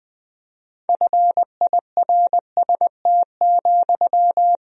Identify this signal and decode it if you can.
Morse code, heard as a keyed tone, about 20 wpm, 700 Hz. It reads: FIRST,
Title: Sentences from Top 1000 Words - Review Part 3 - 20wpm